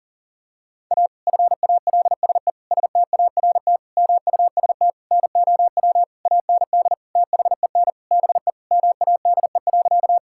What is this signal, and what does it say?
Morse code, heard as a keyed tone, about 37 wpm, 700 Hz. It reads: A FALSE START MUST NOW AND THEN BE MADE.
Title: Sentences from Top 1600 Words 40wpm